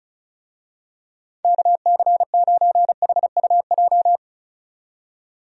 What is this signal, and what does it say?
Morse code, heard as a keyed tone, about 34 wpm, 700 Hz. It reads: KC9HUJ
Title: USA Call Signs - 6 Characters - 35wpm